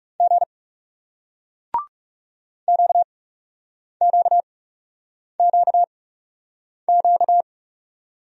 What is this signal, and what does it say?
Morse code, heard as a keyed tone, about 41 wpm, spaced as for 8 wpm, 700 Hz. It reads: G QQQQ